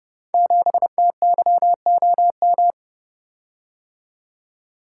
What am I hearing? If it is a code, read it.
Morse code, heard as a keyed tone, about 29 wpm, 700 Hz. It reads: 7TYOM
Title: Code Groups - 5 Letter:Numbers - 30wpm